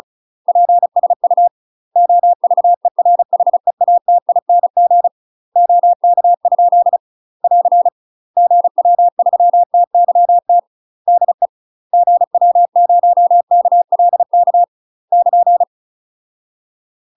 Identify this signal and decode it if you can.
Morse code, heard as a keyed tone, about 36 wpm, 700 Hz. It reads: PSU OVERHEATING OK? <AR> GW3TYT DE GW0KLK <KN>
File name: UK MASTER QSOs TRAINING / 35WPM / UK MASTER QSOs 0036